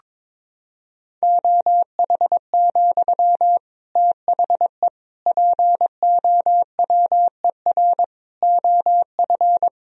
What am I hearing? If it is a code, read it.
Morse code, heard as a keyed tone, about 22 wpm, 700 Hz. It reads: OH, THE POWER OF
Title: Sentences from Top 1000 Words - Review Part 3 - 22wpm